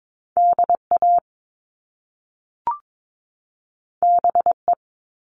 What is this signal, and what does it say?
Morse code, heard as a keyed tone, about 22 wpm, 700 Hz. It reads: DA BE
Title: Top Two Letter Combinations in Words 22wpm